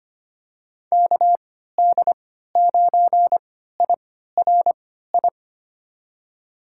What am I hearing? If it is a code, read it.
Morse code, heard as a keyed tone, about 25 wpm, spaced as for 14 wpm, 700 Hz. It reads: KD9IRI
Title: USA Call Signs - 6 Characters - 3x Character Spacing - 25wpm